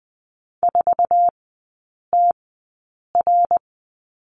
Morse code, heard as a keyed tone, about 20 wpm, spaced as for 8 wpm, 700 Hz. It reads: 4TR
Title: 053 Lesson 10e - Letter r - Three Characters 20wpm